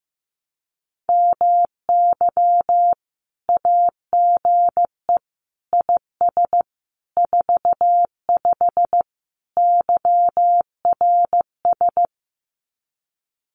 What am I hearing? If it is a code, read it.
Morse code, heard as a keyed tone, about 15 wpm, 700 Hz. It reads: MY AGE IS 45 YRS
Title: Common QSO Phrases Encore 15wpm